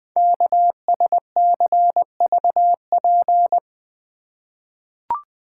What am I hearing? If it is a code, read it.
Morse code, heard as a keyed tone, about 20 wpm, 700 Hz. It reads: KSCVP
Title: Code Groups - 5 Letters - 20wpm